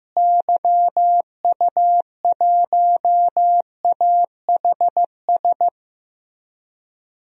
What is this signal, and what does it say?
Morse code, heard as a keyed tone, about 15 wpm, 700 Hz. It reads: YU1AHS
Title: DXCC-100 Call Signs - 6 Characters - Repeated 3x - 15wpm